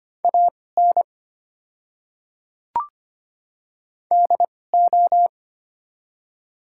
Morse code, heard as a keyed tone, about 25 wpm, spaced as for 18 wpm, 690 Hz. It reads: AN DO
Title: 2x Character Spacing - Top 100 Words - 25wpm